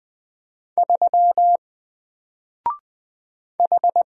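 Morse code, heard as a keyed tone, about 20 wpm, 700 Hz. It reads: 3 H
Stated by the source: Single letter and number Morse drill at 20 wpm, from N4LCD.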